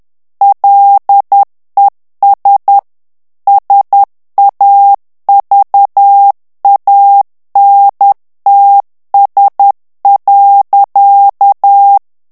Morse code, heard as a keyed tone, about 10 wpm, 800 Hz. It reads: LES SAVANTS.
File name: morse mp3 / FRANCAIS TEXTE CLAIR 2 VITESSE 10 MOTS MINUTES